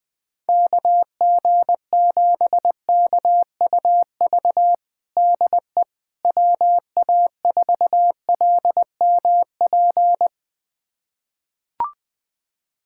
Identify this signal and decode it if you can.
Morse code, heard as a keyed tone, about 21 wpm, 700 Hz. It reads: KG7KUV DE WA4LMP
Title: USA Callsign DE Callsign - 6 Characters - 20wpm